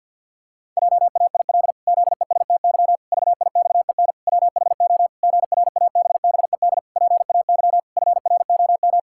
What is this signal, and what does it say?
Morse code, heard as a keyed tone, about 48 wpm, 700 Hz. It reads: JAIL ZESTY VIXEN WHO GRABBED PAY FROM
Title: Pangram Challenge v2 - 50wpm